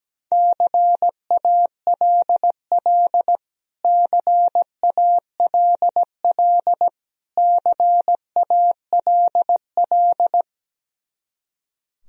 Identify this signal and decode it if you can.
Morse code, heard as a keyed tone, about 17 wpm, 700 Hz. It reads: CALL CALL CALL
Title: Top 100 Words Repeated 17wpm